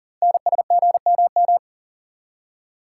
Morse code, heard as a keyed tone, about 40 wpm, 700 Hz. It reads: NSGMM
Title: Code Groups - 5 Letters - 40wpm